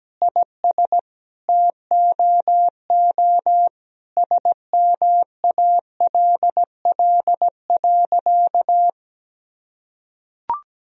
Morse code, heard as a keyed tone, about 17 wpm, 700 Hz. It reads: IS TOO SMALL.
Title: Sentences from Top 1000 Words - Review Part 1 - 17wpm